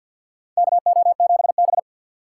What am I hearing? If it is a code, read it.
Morse code, heard as a keyed tone, about 52 wpm, 700 Hz. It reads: KO7B
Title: USA Call Signs - 2x Character Spacing - 50wpm